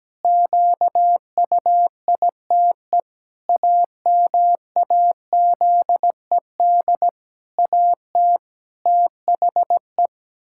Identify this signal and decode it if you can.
Morse code, heard as a keyed tone, about 17 wpm, 700 Hz. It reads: QUITE AMAZED AT THE
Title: Pangram Challenge v2 - 17wpm